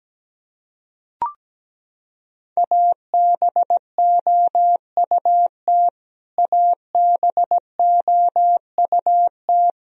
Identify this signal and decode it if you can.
Morse code, heard as a keyed tone, about 17 wpm, 700 Hz. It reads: ABOUT ABOUT
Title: Top 100 Words Repeated 17wpm